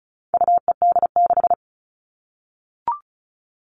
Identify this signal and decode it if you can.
Morse code, heard as a keyed tone, about 34 wpm, 700 Hz. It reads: UED6
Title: All Cut Numbers - 4-Digit 35wpm